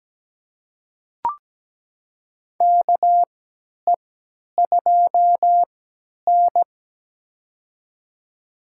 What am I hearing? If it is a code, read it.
Morse code, heard as a keyed tone, about 17 wpm, spaced as for 10 wpm, 700 Hz. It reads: KE2N
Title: USA Call Signs - 3x Character Spacing - 17wpm